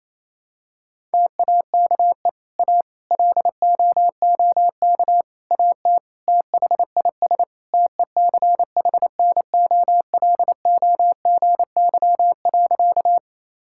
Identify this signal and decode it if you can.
Morse code, heard as a keyed tone, about 28 wpm, 700 Hz. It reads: TAKE A LOOK AT THIS TECHNOLOGY.
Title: Sentences from Top 1000 Words - Review Part 5 - 28wpm